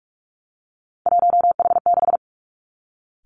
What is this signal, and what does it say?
Morse code, heard as a keyed tone, about 46 wpm, 700 Hz. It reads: 1H6
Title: Three Challenging Letter-Number Combinations 45wpm